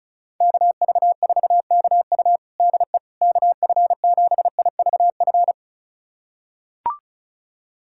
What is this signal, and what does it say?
Morse code, heard as a keyed tone, about 36 wpm, 700 Hz. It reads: KV4KU DE KF7IVF
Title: USA Callsign DE Callsign 35wpm